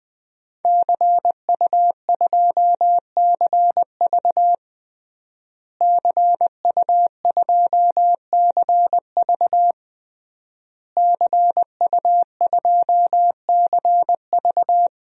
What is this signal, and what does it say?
Morse code, heard as a keyed tone, about 20 wpm, 700 Hz. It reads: CU2CV CU2CV CU2CV
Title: DXCC-100 Call Signs - 5 Characters - Repeated 3x - 20wpm